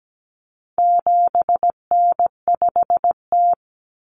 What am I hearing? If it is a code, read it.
Morse code, heard as a keyed tone, about 17 wpm, 700 Hz. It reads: 7N5T
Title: Common Cut Numbers - 4-Digit 17wpm